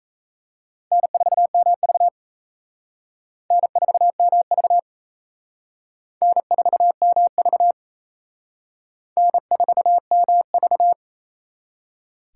Morse code, heard as a keyed tone, about 37 wpm, 700 Hz. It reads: N4MV N4MV N4MV N4MV